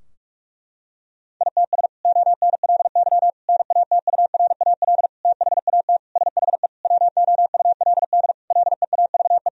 Morse code, heard as a keyed tone, about 46 wpm, 700 Hz. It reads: ITS ONLY NATURAL THAT SHE WOULD LEAVE
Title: Sentences from Top 1000 Words - Review Part 4 - 45wpm